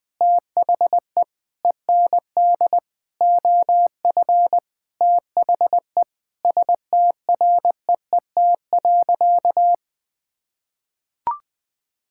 Morse code, heard as a keyed tone, about 20 wpm, 700 Hz. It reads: THE END OF THE STREET.
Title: Sentences with 5-Word Phrases - 20wpm